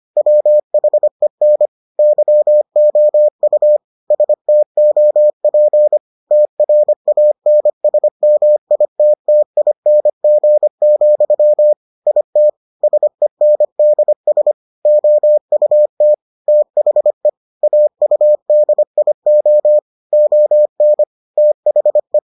Morse code, heard as a keyed tone, about 25 wpm, 600 Hz. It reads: WHEN YOU STOP TRANSMITTING, IT SENDS OUT THE AUDIO ON THE